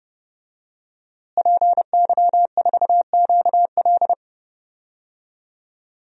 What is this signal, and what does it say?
Morse code, heard as a keyed tone, about 32 wpm, 700 Hz. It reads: PY4QL
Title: DXCC-100 Call Signs - 5 Characters - Repeated 2x - 30wpm